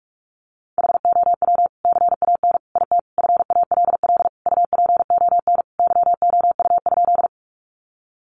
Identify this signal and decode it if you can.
Morse code, heard as a keyed tone, about 44 wpm, 700 Hz. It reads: HOW CAN IT FALL UPON YOU?
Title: Sentences from Top 1100 Words 45wpm